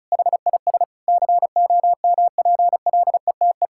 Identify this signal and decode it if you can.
Morse code, heard as a keyed tone, about 34 wpm, 700 Hz. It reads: HIS COMPLETE
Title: Sentences from Top 1000 Words - Review Part 6 - 35wpm